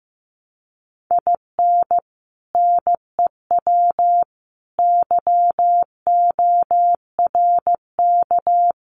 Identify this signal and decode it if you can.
Morse code, heard as a keyed tone, about 15 wpm, 700 Hz. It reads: IN NEW YORK
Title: Sets of 4 Words 15wpm